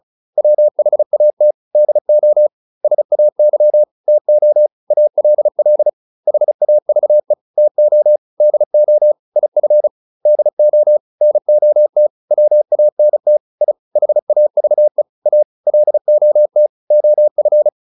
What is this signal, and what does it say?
Morse code, heard as a keyed tone, about 34 wpm, 600 Hz. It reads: WHAT DO SAY TO ALL HAVE TO DO IF DO NOT WANT I HAVE A LOT OF